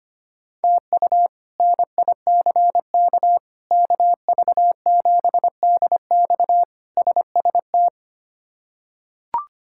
Morse code, heard as a keyed tone, about 25 wpm, 700 Hz. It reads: TU NICK KV7DX SST
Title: K1USN SST QSO Practice - 25wpm